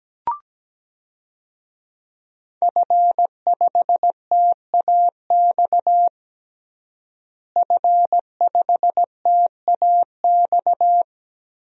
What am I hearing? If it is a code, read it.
Morse code, heard as a keyed tone, about 17 wpm, 700 Hz. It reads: F5TAX F5TAX
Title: DXCC-100 Call Signs - 5 Characters - Repeated 2x - 17wpm